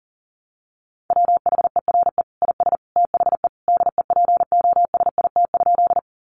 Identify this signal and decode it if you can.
Morse code, heard as a keyed tone, about 40 wpm, 700 Hz. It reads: WHERE IS THE DEPOSIT?
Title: Sentences from Top 1300 Words 40wpm